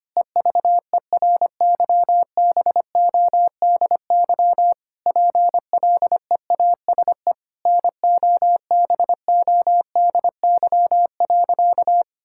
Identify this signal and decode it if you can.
Morse code, heard as a keyed tone, about 25 wpm, 700 Hz. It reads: EVERYBODY PLEASE NOBODY.
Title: Sentences from Top 1600 Words 25wpm